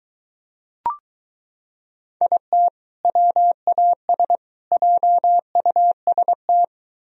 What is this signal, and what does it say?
Morse code, heard as a keyed tone, about 23 wpm, 700 Hz. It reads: IT WAS JUST